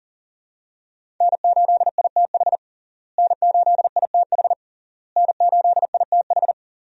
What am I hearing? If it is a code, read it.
Morse code, heard as a keyed tone, about 40 wpm, 700 Hz. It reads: N8ITH N8ITH N8ITH
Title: USA Call Signs - 5 Characters - Repeated 3x - 40wpm